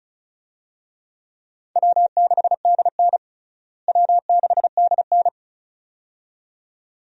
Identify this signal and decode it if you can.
Morse code, heard as a keyed tone, about 36 wpm, 700 Hz. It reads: W6DN W6DN
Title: USA Call Signs - 4 Characters - Repeated 2x - 35wpm